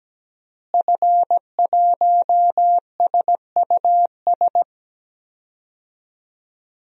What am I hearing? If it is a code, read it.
Morse code, heard as a keyed tone, about 17 wpm, 700 Hz. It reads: F1SUS
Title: DXCC-100 Call Signs - 5 Characters - Repeated 3x - 17wpm